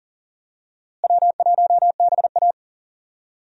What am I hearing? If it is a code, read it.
Morse code, heard as a keyed tone, about 40 wpm, 700 Hz. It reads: W1BA
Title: USA Call Signs - 4 Characters - Repeated 2x - 40wpm